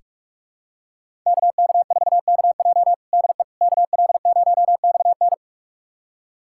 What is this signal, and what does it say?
Morse code, heard as a keyed tone, about 46 wpm, 700 Hz. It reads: KK4KJ DE KL0XN
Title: USA Callsign DE Callsign - 5 Characters - 45wpm